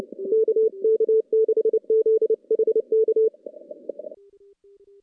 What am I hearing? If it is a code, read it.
Morse code, heard as a keyed tone, about 28 wpm, 410 Hz. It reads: KK6ZHK
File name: contests / CW-SS-k5tr- -2300